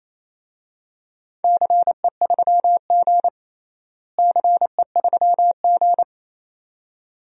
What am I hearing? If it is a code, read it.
Morse code, heard as a keyed tone, about 27 wpm, 700 Hz. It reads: CE3G CE3G
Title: DXCC-100 Call Signs - 4 Characters - Repeated 2x - 28wpm